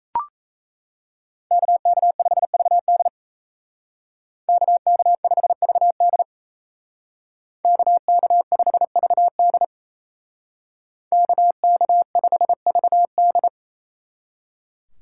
Morse code, heard as a keyed tone, about 35 wpm, 700 Hz. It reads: KK5VD KK5VD KK5VD KK5VD